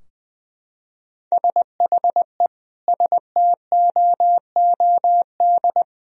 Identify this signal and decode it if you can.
Morse code, heard as a keyed tone, about 20 wpm, 700 Hz. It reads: SHE STOOD
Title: Sentences from Top 1000 Words - Review Part 3 - 20wpm